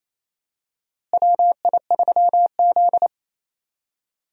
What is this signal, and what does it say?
Morse code, heard as a keyed tone, about 28 wpm, 700 Hz. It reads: WI3Z
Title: USA Call Signs - 4 Characters - Repeated 3x - 28wpm